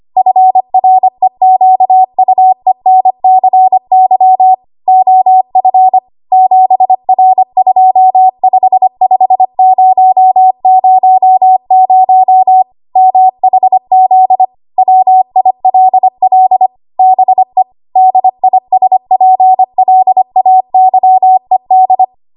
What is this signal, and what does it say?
Morse code, heard as a keyed tone, about 25 wpm, 750 Hz. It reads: FREQUENCY OF 7R255000 MHZ WILL BE DISPLAYED
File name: Archive / 25 WPM / 25WPM